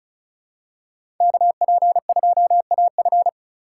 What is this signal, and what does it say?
Morse code, heard as a keyed tone, about 34 wpm, 700 Hz. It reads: KP2AF
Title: USA Call Signs - 5 Characters - 3x Character Spacing - 35wpm